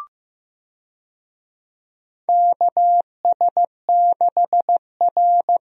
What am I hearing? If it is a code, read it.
Morse code, heard as a keyed tone, about 15 wpm, 700 Hz. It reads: KS6R
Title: USA Call Signs - 4 Characters - Repeated 2x - 15wpm